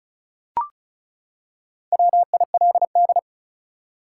Morse code, heard as a keyed tone, about 36 wpm, 700 Hz. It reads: WILD
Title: Four Letter Words - 35wpm